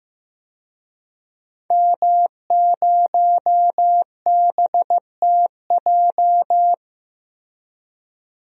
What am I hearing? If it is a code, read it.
Morse code, heard as a keyed tone, about 15 wpm, 700 Hz. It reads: M0BTJ
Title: DXCC-100 Call Signs - 5 Characters - Repeated 3x - 15wpm